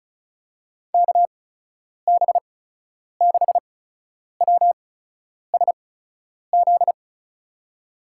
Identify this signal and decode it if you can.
Morse code, heard as a keyed tone, about 36 wpm, spaced as for 10 wpm, 700 Hz. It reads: KB6WSZ
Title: USA Call Signs - 6 Characters - 8x Character Spacing - 35wpm